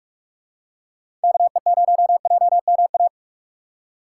Morse code, heard as a keyed tone, about 46 wpm, 700 Hz. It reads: KE0JMA
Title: USA Call Signs - 6 Characters - 6x Character Spacing - 45wpm